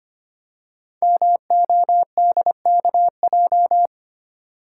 Morse code, heard as a keyed tone, about 25 wpm, 700 Hz. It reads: MODKJ
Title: Code Groups - 5 Letter:Numbers - 25wpm